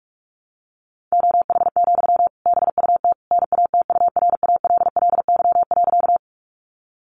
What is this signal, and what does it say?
Morse code, heard as a keyed tone, about 44 wpm, 700 Hz. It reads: OH, BUT NATURALLY.
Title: Sentences from Top 1300 Words 45wpm